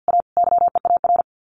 Morse code, heard as a keyed tone, about 50 wpm, 700 Hz. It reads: A YEAR
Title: Sets of 4 Words 50wpm